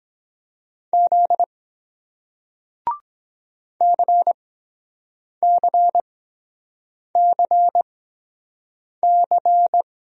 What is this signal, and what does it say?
Morse code, heard as a keyed tone, about 23 wpm, spaced as for 7 wpm, 700 Hz. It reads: Z CCCC